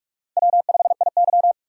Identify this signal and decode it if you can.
Morse code, heard as a keyed tone, about 46 wpm, 700 Hz. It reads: W5IY
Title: USA Call Signs - 4 Characters - 6x Character Spacing - 45wpm